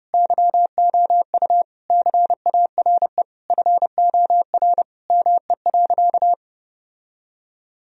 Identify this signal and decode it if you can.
Morse code, heard as a keyed tone, about 32 wpm, 700 Hz. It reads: YOU CARE FOR ME.
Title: Sentences from Top 1000 Words - Review Part 1 - 30wpm